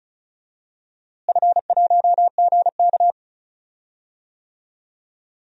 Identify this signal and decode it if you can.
Morse code, heard as a keyed tone, about 36 wpm, 700 Hz. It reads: F1GK
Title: DXCC-100 Call Signs - 4 Characters - Repeated 3x - 35wpm